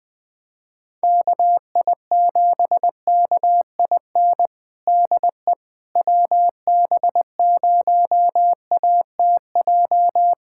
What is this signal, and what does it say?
Morse code, heard as a keyed tone, about 20 wpm, 700 Hz. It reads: KI7KIN DE WB0ATJ
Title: USA Callsign DE Callsign 20wpm